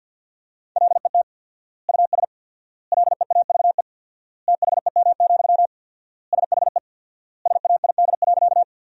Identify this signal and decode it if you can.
Morse code, heard as a keyed tone, about 50 wpm, 700 Hz. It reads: LET US LEAVE THEM, SHE SAID.
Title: 4x Word-Spacing - Sentences from Top 400 Words - 50wpm